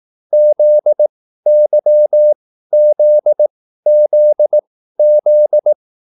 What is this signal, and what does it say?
Morse code, heard as a keyed tone, about 18 wpm, spaced as for 13 wpm, 600 Hz. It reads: ZYZZZ